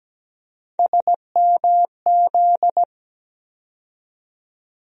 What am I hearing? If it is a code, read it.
Morse code, heard as a keyed tone, about 17 wpm, 700 Hz. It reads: SMZ
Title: USA Call Sign Postfixes - 3 Letters 17wpm